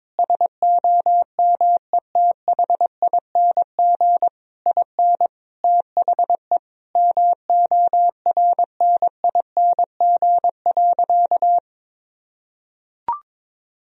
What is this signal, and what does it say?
Morse code, heard as a keyed tone, about 22 wpm, 700 Hz. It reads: SOMETHING IN THE MORNING.
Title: Sentences from Top 1000 Words - Review Part 4 - 22wpm